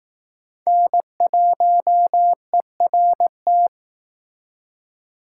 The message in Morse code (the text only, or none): N1ERT